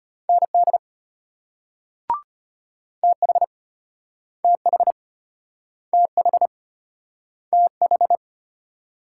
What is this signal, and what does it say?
Morse code, heard as a keyed tone, about 30 wpm, 700 Hz. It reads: ND TH TH TH TH
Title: Speed-Racing Multi-letter Phonemes - 25wpm